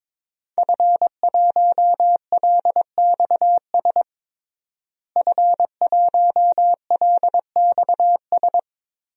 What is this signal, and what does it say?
Morse code, heard as a keyed tone, about 22 wpm, 700 Hz. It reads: F1LXS F1LXS
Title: DXCC-100 Call Signs - 5 Characters - Repeated 3x - 22wpm